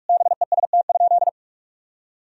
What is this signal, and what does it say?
Morse code, heard as a keyed tone, about 42 wpm, 700 Hz. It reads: BEST?
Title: Sentences from Top 1000 Words - Review Part 2 - 45wpm